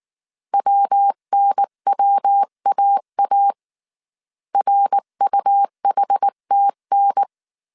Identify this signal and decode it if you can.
Morse code, heard as a keyed tone, about 19 wpm, 800 Hz. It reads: WNWAA RUHTN
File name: CW / DET / 1 3 1500C S B 18wpm 1916